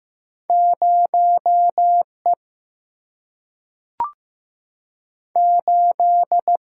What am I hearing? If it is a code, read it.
Morse code, heard as a keyed tone, about 15 wpm, 700 Hz. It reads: 0 E 8
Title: Single-Letter Single-Number 15wpm